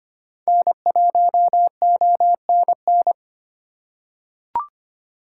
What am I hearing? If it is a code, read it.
Morse code, heard as a keyed tone, about 25 wpm, 700 Hz. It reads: N1ONN
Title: USA Call Signs - 5 Characters - 4x Character Spacing - 25wpm